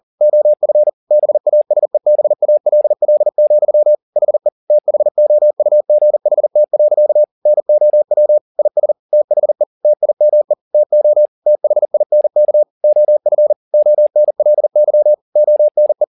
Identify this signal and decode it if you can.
Morse code, heard as a keyed tone, about 40 wpm, 600 Hz. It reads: OF BASEBALL, HE THOUGHT. NOW IS THE TIME TO THINK OF ONLY ONE